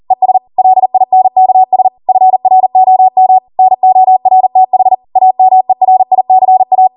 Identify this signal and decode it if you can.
Morse code, heard as a keyed tone, about 40 wpm, 750 Hz. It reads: ES LINKS FROM NORTH AMERICA